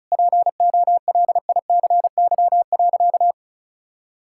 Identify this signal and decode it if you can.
Morse code, heard as a keyed tone, about 34 wpm, 700 Hz. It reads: POLICY.
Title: Sentences from Top 1000 Words - Review Part 5 - 35wpm